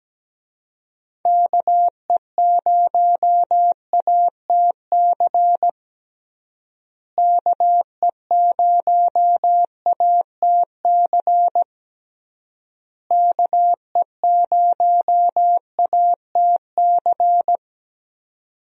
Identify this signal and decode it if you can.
Morse code, heard as a keyed tone, about 17 wpm, 700 Hz. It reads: KE0ATC KE0ATC KE0ATC
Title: USA Call Signs - 6 Characters - Repeated 3x - 17wpm